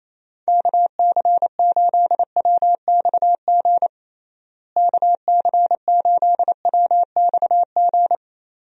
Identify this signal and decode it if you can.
Morse code, heard as a keyed tone, about 28 wpm, 700 Hz. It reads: KC8WXG KC8WXG
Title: USA Call Signs - 6 Characters - Repeated 2x - 28wpm